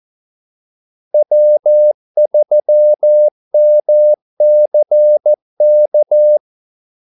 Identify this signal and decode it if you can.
Morse code, heard as a keyed tone, about 14 wpm, 600 Hz. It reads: W3MCK
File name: Callsigns-14wpm